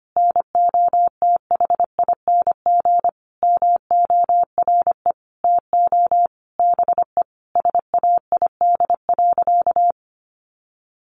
Morse code, heard as a keyed tone, about 25 wpm, 700 Hz. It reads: NOTHING MORE TO BE SAID.
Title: Sentences from Top 1000 Words 25wpm